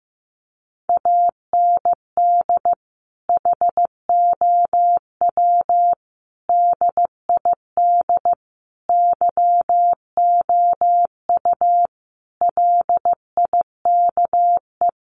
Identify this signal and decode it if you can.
Morse code, heard as a keyed tone, about 15 wpm, 700 Hz. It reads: AND HOW DID YOU LIKE
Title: Sentences from Top 1300 Words 15wpm